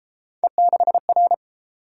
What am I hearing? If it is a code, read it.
Morse code, heard as a keyed tone, about 31 wpm, 700 Hz. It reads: E6R